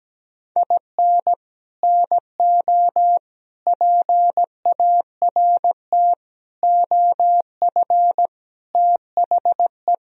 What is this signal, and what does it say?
Morse code, heard as a keyed tone, about 17 wpm, 700 Hz. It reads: IN NO PART OF THE
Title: Sentences from Top 1000 Words - Review Part 3 - 17wpm